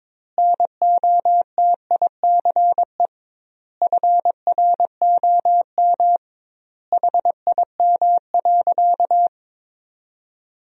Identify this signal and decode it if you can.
Morse code, heard as a keyed tone, about 22 wpm, 700 Hz. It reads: NOTICE FROM HIM.
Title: 2x Word-Spacing - Sentences from Top 400 Words - 22wpm